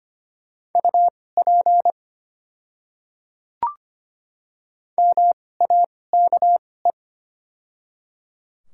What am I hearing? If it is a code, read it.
Morse code, heard as a keyed tone, about 25 wpm, spaced as for 18 wpm, 700 Hz. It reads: UP MAKE
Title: 2x Character Spacing - Top 100 Words - 25wpm